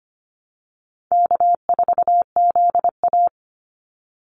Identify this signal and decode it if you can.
Morse code, heard as a keyed tone, about 25 wpm, 700 Hz. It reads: K4ZA
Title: Easy Call Signs - 25wpm